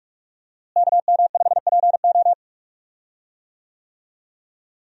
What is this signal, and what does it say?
Morse code, heard as a keyed tone, about 42 wpm, 700 Hz. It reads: KM5PO
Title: USA Call Signs - 5 Characters - Repeated 2x - 45wpm